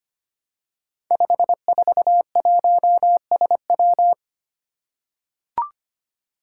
Morse code, heard as a keyed tone, about 25 wpm, 700 Hz. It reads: 541SW
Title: Five Challenging Letter-Number Combinations 25wpm